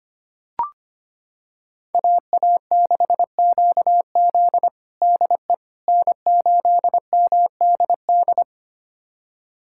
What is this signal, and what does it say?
Morse code, heard as a keyed tone, about 25 wpm, 700 Hz. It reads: AA6QZ DE N8MDD
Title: USA Callsign DE Callsign - 5 Characters - 25wpm